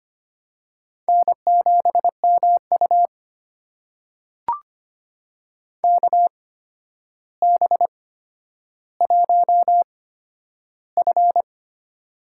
Morse code, heard as a keyed tone, about 25 wpm, spaced as for 11 wpm, 700 Hz. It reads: N7MU KB1F
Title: USA Call Signs - 8x Character Spacing - 25wpm